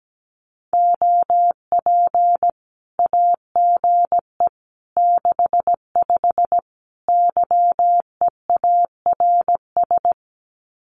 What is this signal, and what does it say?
Morse code, heard as a keyed tone, about 17 wpm, 700 Hz. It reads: OP AGE 65 YEARS
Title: Common QSO Phrases 17wpm